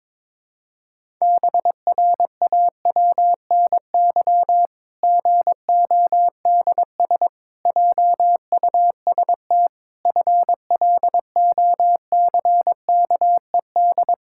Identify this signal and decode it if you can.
Morse code, heard as a keyed tone, about 22 wpm, 700 Hz. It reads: BRAWNY GODS JUST FLOCKED
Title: Pangram Challenge v2 - 22wpm